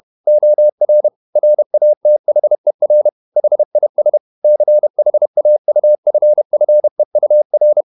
Morse code, heard as a keyed tone, about 31 wpm, 600 Hz. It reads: OR RATHER HIS CHAUFFEUR
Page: Never 31wpm | CW med Gnister